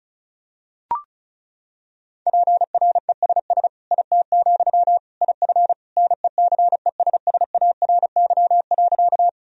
Morse code, heard as a keyed tone, about 36 wpm, 700 Hz. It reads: PRESS IT, IF NECESSARY.
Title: Sentences from Top 1000 Words - Review Part 1 - 35wpm